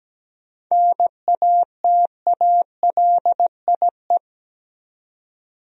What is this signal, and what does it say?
Morse code, heard as a keyed tone, about 17 wpm, 700 Hz. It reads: NATALIE
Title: USA Common Names - 17wpm